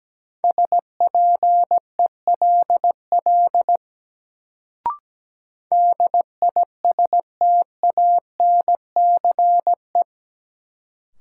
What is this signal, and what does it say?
Morse code, heard as a keyed tone, about 17 wpm, 700 Hz. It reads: SPELL DISTANCE
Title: Sentences from Top 1200 Words - New Word Drill Alt - 17wpm